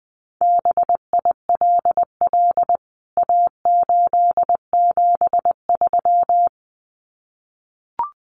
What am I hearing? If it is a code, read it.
Morse code, heard as a keyed tone, about 20 wpm, 700 Hz. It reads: BILL A873
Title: CWT Exchange 20wpm